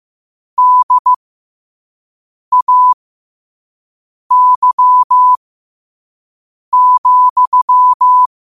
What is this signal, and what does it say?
Morse code, heard as a keyed tone, about 15 wpm, spaced as for 5 wpm, 1000 Hz. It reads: DAY,
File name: qotdfiles / 5wpm